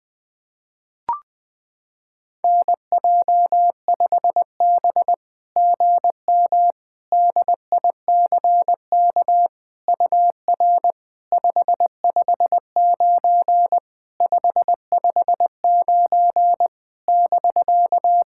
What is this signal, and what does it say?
Morse code, heard as a keyed tone, about 21 wpm, 700 Hz. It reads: NJ5B GM DICK UR 559 559 <BK>
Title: SOTA QSO Practice 20wpm